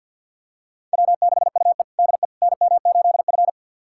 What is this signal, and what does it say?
Morse code, heard as a keyed tone, about 52 wpm, 700 Hz. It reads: W6UE DE NM8F
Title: USA Callsign DE Callsign - 4 Characters - 50wpm